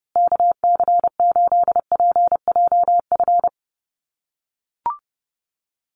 Morse code, heard as a keyed tone, about 30 wpm, 700 Hz. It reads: KC8PJF
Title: USA Call Signs Repeated 3x 30wpm